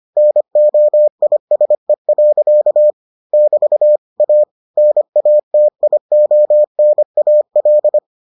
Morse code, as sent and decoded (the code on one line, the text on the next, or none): -. --- .. ... .-.-.- / -...- / .- / -. .- - .. --- -. .- .-..
NOISE. <BT> A NATIONAL